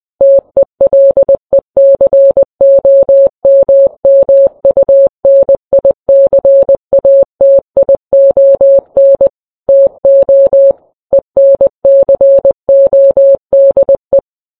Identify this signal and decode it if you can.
Morse code, heard as a keyed tone, about 20 wpm, 550 Hz. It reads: TELECOMMUNICATION TO ENCODE